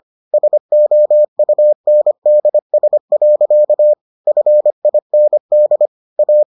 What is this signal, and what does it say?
Morse code, heard as a keyed tone, about 25 wpm, 600 Hz. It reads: SOUNDS. FIND A